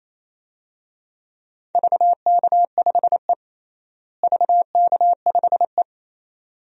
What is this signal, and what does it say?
Morse code, heard as a keyed tone, about 28 wpm, 700 Hz. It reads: VK5E VK5E
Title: DXCC-100 Call Signs - 4 Characters - Repeated 2x - 28wpm